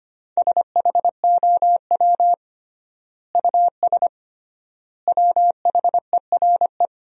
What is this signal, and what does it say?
Morse code, heard as a keyed tone, about 25 wpm, 700 Hz. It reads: SHOW US WHERE
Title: 3x Word-Spacing - Sentences from Top 200 Words - 25wpm